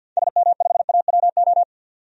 Morse code, heard as a keyed tone, about 50 wpm, 700 Hz. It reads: SM5AWO
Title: DXCC-100 Call Signs - 6 Characters - 50wpm